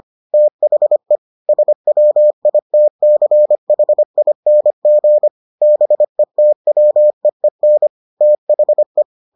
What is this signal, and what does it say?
Morse code, heard as a keyed tone, about 25 wpm, 600 Hz. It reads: THE SWITCHING BETWEEN THE